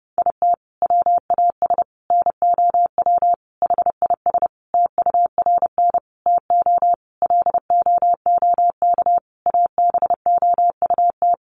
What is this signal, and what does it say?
Morse code, heard as a keyed tone, about 30 wpm, 700 Hz. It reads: IT WAS NOW HIS TURN TO LOOK ABOUT
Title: Sentences from Top 200 Words 30wpm